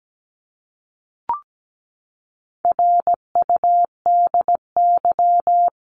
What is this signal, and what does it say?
Morse code, heard as a keyed tone, about 17 wpm, 700 Hz. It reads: RUDY
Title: CWT Exchange 17wpm